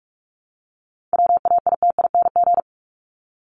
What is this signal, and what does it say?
Morse code, heard as a keyed tone, about 44 wpm, 700 Hz. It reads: WAITING
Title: ING Suffix - Top 100 Words with Base Words 45wpm